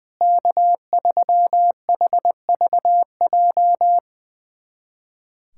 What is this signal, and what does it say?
Morse code, heard as a keyed tone, about 21 wpm, 700 Hz. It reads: K3HVJ
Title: USA Callsign DE Callsign - 5 Characters - 20wpm